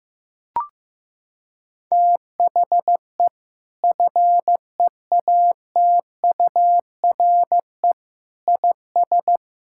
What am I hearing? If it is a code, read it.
Morse code, heard as a keyed tone, about 15 wpm, 700 Hz. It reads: THE FEATURE IS
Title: Sentences from Top 1000 Words - Review Part 5 - 15wpm